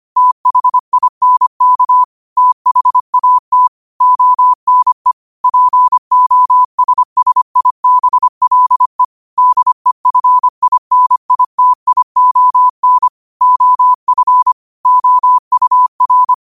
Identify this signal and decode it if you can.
Morse code, heard as a keyed tone, about 25 wpm, 1000 Hz. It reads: THINK THAT ONE POSSIBLE DEFINITION OF OUR
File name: qotdfiles / 25wpm